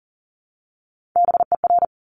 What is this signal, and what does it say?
Morse code, heard as a keyed tone, about 38 wpm, 690 Hz. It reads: BER
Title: Top Three Letter Combinations in Words 40wpm